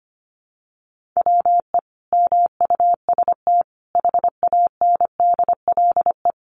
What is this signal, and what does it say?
Morse code, heard as a keyed tone, about 25 wpm, 700 Hz. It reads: WE MUST HANDLE
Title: Sentences from Top 1100 Words 25wpm